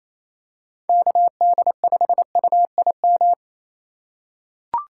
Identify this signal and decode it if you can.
Morse code, heard as a keyed tone, about 28 wpm, 700 Hz. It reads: KD5UIM
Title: USA Call Signs - 6 Characters - 2x Character Spacing - 28wpm